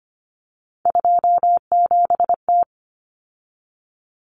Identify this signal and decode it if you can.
Morse code, heard as a keyed tone, about 25 wpm, 700 Hz. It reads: 27T
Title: Common Cut Numbers - 3-Digit 25wpm